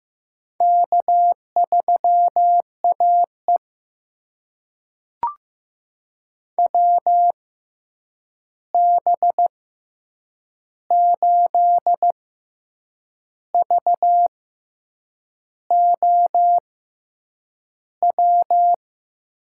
Morse code, heard as a keyed tone, about 15 wpm, spaced as for 7 wpm, 700 Hz. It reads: K3AE WB8VOW
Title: USA Call Signs - 6x Character Spacing - 15wpm